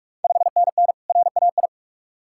Morse code, heard as a keyed tone, about 46 wpm, 690 Hz. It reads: 5NN RAI
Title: Ontario QSO Party Exchange 45wpm